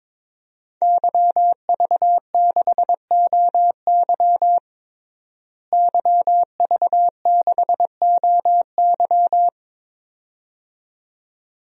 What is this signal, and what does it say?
Morse code, heard as a keyed tone, about 22 wpm, 700 Hz. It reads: YV6OY YV6OY
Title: DXCC-100 Call Signs - 5 Characters - Repeated 2x - 22wpm